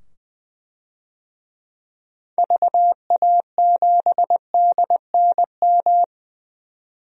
Morse code, heard as a keyed tone, about 20 wpm, 700 Hz. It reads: VA7DNM
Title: DXCC-100 Call Signs - 6 Characters - 20wpm